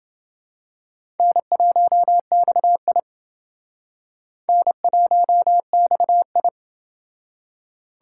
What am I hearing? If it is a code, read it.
Morse code, heard as a keyed tone, about 29 wpm, 700 Hz. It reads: N1XI N1XI